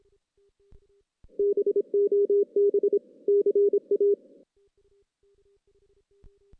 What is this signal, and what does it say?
Morse code, heard as a keyed tone, about 26 wpm, 400 Hz. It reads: BOB CA
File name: 2016 JAN CW NAQP / 2016 Jan naqp k5ot - 0-6